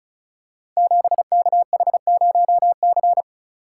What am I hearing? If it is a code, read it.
Morse code, heard as a keyed tone, about 36 wpm, 700 Hz. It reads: 7KH0C
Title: Code Groups - 5 Letter:Numbers - 35wpm